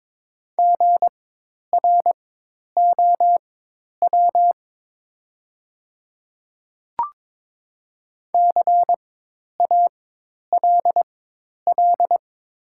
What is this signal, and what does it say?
Morse code, heard as a keyed tone, about 22 wpm, spaced as for 10 wpm, 700 Hz. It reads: GROW CALL
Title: 4x Character Spacing - Top 300 Words - 22wpm